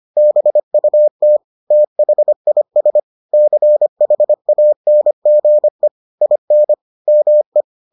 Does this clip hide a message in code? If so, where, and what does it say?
Yes: Morse code from 0.2 to 7.6 s, heard as a keyed tone, about 25 wpm, 600 Hz. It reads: BUT THIS CHANGE IN ME